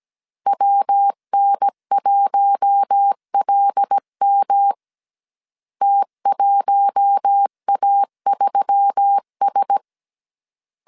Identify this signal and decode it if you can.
Morse code, heard as a keyed tone, about 17 wpm, 800 Hz. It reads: WN1LM T1A3S
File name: 16wpm